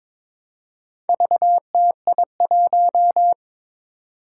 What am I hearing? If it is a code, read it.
Morse code, heard as a keyed tone, about 22 wpm, 700 Hz. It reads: VTI1
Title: Four Challenging Letter-Number Combinations 22wpm